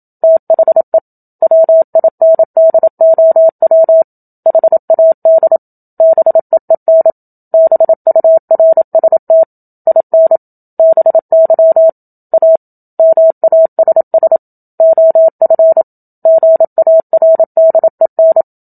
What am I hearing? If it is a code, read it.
Morse code, heard as a keyed tone, about 27 wpm, 650 Hz. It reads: THE WINDOW HAD BEEN BURST IN BY A MASS OF GARDEN
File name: Book Two / 01-Chapter 1 - 27 WPM